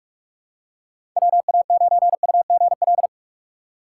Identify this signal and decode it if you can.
Morse code, heard as a keyed tone, about 47 wpm, 700 Hz. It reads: WA9UGL
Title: USA Call Signs - 6 Characters - 6x Character Spacing - 45wpm